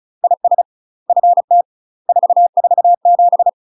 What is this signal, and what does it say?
Morse code, heard as a keyed tone, about 34 wpm, 700 Hz. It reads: IS FT 447
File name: UK MASTER QSOs TRAINING / 35WPM / UK MASTER QSOs 0037